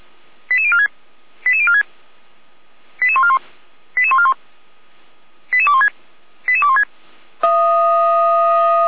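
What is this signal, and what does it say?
Five-tone selective call (ZVEI), heard as quick runs of five tones, tones of about 70 ms. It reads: 88046 88046 88141 88141 88216 88216
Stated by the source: Five-tone ZVEI selective call received on SDR, from the feuerwehr-orlamuende.org fire brigade server.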